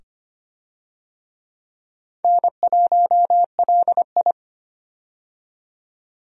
Morse code, heard as a keyed tone, about 25 wpm, 700 Hz. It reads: N1LI
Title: USA Call Signs - 4 Characters - 25wpm